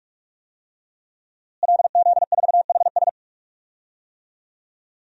Morse code, heard as a keyed tone, about 46 wpm, 700 Hz. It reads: LZ4HS
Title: DXCC-100 Call Signs - 5 Characters - Repeated 3x - 45wpm